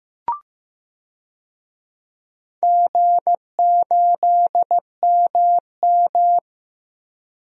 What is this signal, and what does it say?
Morse code, heard as a keyed tone, about 15 wpm, 700 Hz. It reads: G8MM
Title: DXCC-100 Call Signs - 4 Characters - Repeated 2x - 15wpm